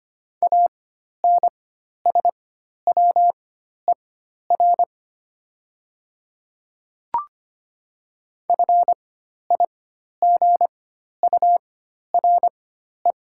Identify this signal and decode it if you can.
Morse code, heard as a keyed tone, about 25 wpm, spaced as for 12 wpm, 700 Hz. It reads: ANSWER FIGURE
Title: 4x Character Spacing - Top 500 Words - 25wpm